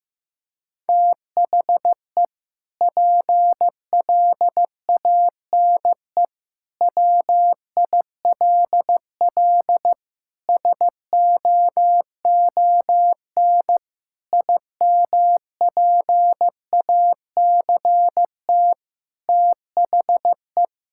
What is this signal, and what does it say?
Morse code, heard as a keyed tone, about 15 wpm, 700 Hz. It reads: THE PLANE WILL SOON IMPACT THE
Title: Sentences from Top 1000 Words - Review Part 6 - 15wpm